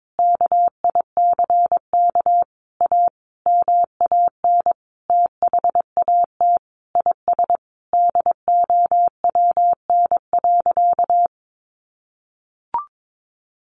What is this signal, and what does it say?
Morse code, heard as a keyed tone, about 22 wpm, 700 Hz. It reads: KICK A MAN THAT IS DOWN.
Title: Sentences from Top 1300 Words 22wpm